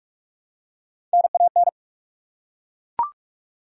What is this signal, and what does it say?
Morse code, heard as a keyed tone, about 42 wpm, 700 Hz. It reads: NAN
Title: USA Common Names - Max 3 Letters - 45wpm